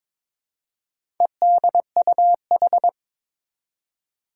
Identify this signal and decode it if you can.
Morse code, heard as a keyed tone, about 23 wpm, 700 Hz. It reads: EDUH
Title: Four Challenging Letter-Number Combinations 22wpm